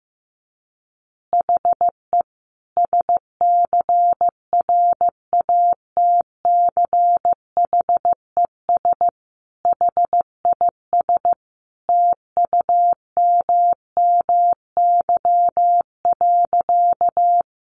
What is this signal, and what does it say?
Morse code, heard as a keyed tone, about 15 wpm, 700 Hz. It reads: HE SCRATCHES HIS TUMMY.
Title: Easy Phrases 15wpm